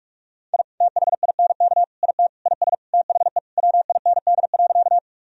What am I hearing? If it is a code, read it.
Morse code, heard as a keyed tone, about 46 wpm, 700 Hz. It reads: I THINK IT IS THE WIND.
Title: Sentences from Top 1000 Words - Review Part 2 - 45wpm